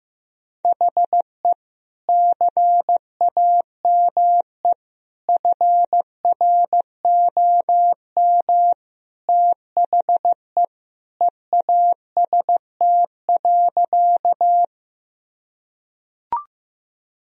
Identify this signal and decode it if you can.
Morse code, heard as a keyed tone, about 15 wpm, 700 Hz. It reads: HE CAME FROM THE EAST.
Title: Sentences from Top 1000 Words - Review Part 3 - 15wpm